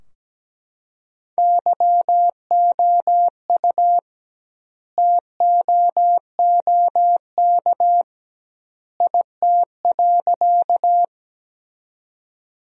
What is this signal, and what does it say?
Morse code, heard as a keyed tone, about 17 wpm, 700 Hz. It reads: YOU TOOK IT.
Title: 2x Word-Spacing - Sentences from Top 300 Words - 17wpm